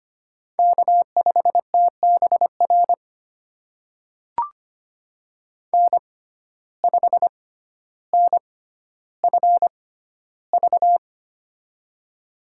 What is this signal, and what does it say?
Morse code, heard as a keyed tone, about 25 wpm, spaced as for 13 wpm, 700 Hz. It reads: K5TBR N5NFV
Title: USA Call Signs - 5 Characters - 6x Character Spacing - 25wpm